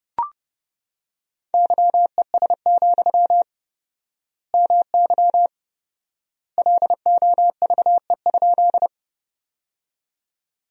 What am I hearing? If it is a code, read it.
Morse code, heard as a keyed tone, about 30 wpm, 700 Hz. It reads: YES, MY LOVE?
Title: 4x Word-Spacing - Sentences from Top 500 Words - 30wpm